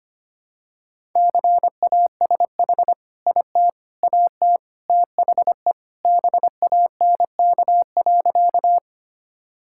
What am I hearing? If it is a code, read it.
Morse code, heard as a keyed tone, about 25 wpm, 700 Hz. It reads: CASH IT AT THE BANK.
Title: Sentences from Top 1000 Words - Review Part 5 - 25wpm